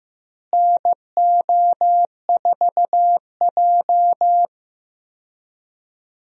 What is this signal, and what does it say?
Morse code, heard as a keyed tone, about 15 wpm, 700 Hz. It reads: NO4J
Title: USA Call Signs - 4 Characters - Repeated 3x - 15wpm